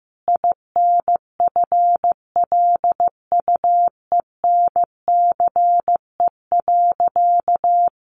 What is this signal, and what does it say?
Morse code, heard as a keyed tone, about 14 wpm, 700 Hz. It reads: INFLUENCE.
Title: Sentences from Top 1100 Words 15wpm